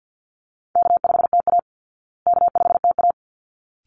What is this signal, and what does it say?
Morse code, heard as a keyed tone, about 48 wpm, 700 Hz. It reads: K5TA K5TA
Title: Easy Call Signs Repeated 2x - 50wpm